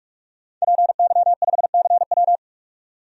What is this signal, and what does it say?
Morse code, heard as a keyed tone, about 45 wpm, 700 Hz. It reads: PY5CW
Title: DXCC-100 Call Signs - 5 Characters - 45wpm